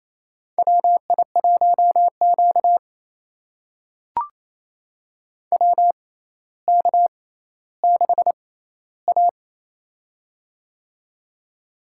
Morse code, heard as a keyed tone, about 28 wpm, spaced as for 14 wpm, 700 Hz. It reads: WI1Q WK6A
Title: USA Call Signs - 4 Characters - 6x Character Spacing - 28wpm